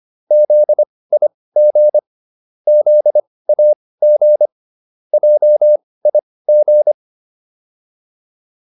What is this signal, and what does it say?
Morse code, heard as a keyed tone, about 25 wpm, spaced as for 18 wpm, 600 Hz. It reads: ZIG ZAG JIG